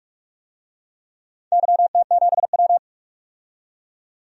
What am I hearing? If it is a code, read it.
Morse code, heard as a keyed tone, about 44 wpm, 700 Hz. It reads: YT7W
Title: DXCC-100 Call Signs - 4 Characters - 45wpm